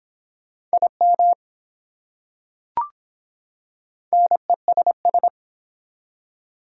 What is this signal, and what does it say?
Morse code, heard as a keyed tone, about 24 wpm, 700 Hz. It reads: IM NESS